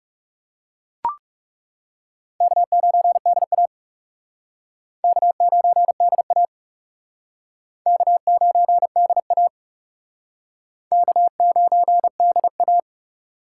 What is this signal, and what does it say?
Morse code, heard as a keyed tone, about 38 wpm, 700 Hz. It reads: K9DA K9DA K9DA K9DA